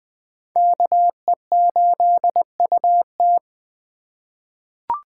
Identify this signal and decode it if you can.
Morse code, heard as a keyed tone, about 21 wpm, 700 Hz. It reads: KE8UT
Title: USA Call Signs - 2x Character Spacing - 20wpm